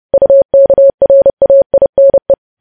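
Morse code, heard as a keyed tone, about 30 wpm, 550 Hz. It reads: UKRAINE
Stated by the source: News headlines sent in Morse code at 30 wpm.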